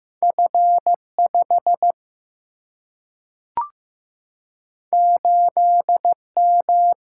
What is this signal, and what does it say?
Morse code, heard as a keyed tone, about 15 wpm, 700 Hz. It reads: F 5 8 M
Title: Single-Letter Single-Number 15wpm